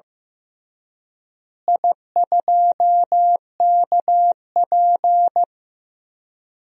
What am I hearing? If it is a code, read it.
Morse code, heard as a keyed tone, about 15 wpm, 700 Hz. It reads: I2KP
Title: DXCC-100 Call Signs - 4 Characters - Repeated 3x - 15wpm